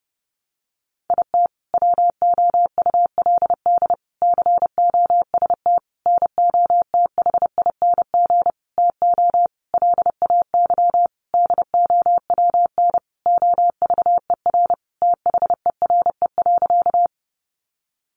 Morse code, heard as a keyed tone, about 30 wpm, 700 Hz. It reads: IT WOULD COST NOTHING TO LAY DOWN OVER THERE.
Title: Sentences from Top 800 Words 30wpm